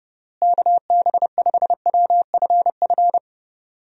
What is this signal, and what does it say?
Morse code, heard as a keyed tone, about 30 wpm, 700 Hz. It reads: KB5WFF
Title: USA Call Signs - 6 Characters - Repeated 3x - 30wpm